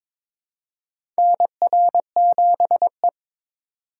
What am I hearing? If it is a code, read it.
Morse code, heard as a keyed tone, about 22 wpm, 700 Hz. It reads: NR7E
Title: USA Call Signs - 4 Characters - 3x Character Spacing - 22wpm